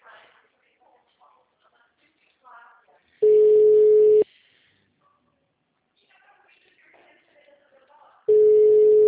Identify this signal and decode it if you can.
Ringback tone (Europe, 425 Hz), with speech.